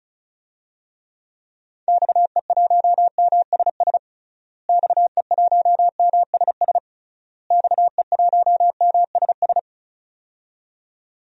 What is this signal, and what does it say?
Morse code, heard as a keyed tone, about 36 wpm, 700 Hz. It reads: XE1MSS XE1MSS XE1MSS
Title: DXCC-100 Call Signs - 6 Characters - Repeated 3x - 35wpm